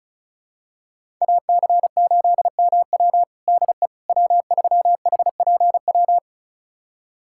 Morse code, heard as a keyed tone, about 34 wpm, 700 Hz. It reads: AC8MW DE W3HPW
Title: USA Callsign DE Callsign - 5 Characters - 35wpm